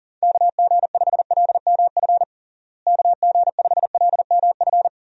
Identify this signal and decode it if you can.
Morse code, heard as a keyed tone, about 40 wpm, 700 Hz. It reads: KG5LMF KG5LMF
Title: USA Call Signs - 6 Characters - Repeated 2x - 40wpm